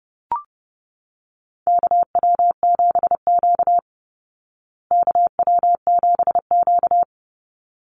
Morse code, heard as a keyed tone, about 30 wpm, 700 Hz. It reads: KW7Q KW7Q
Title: Call Signs Repeated 3x - 30wpm